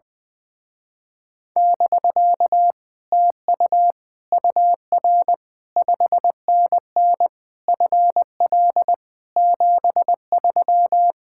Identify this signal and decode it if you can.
Morse code, heard as a keyed tone, about 20 wpm, 700 Hz. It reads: <BK> TU UR 5NN FL 73
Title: Speed-Racing Common QSO Phrases Encore - 20wpm